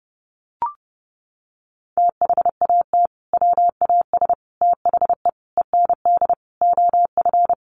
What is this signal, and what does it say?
Morse code, heard as a keyed tone, about 30 wpm, 700 Hz. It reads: THAT WAS THE END OF
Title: Sentences from Top 1400 Words 30wpm